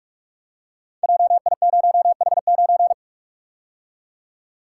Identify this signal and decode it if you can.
Morse code, heard as a keyed tone, about 42 wpm, 700 Hz. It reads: JI0H9
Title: Code Groups - 5 Letter:Numbers - 45wpm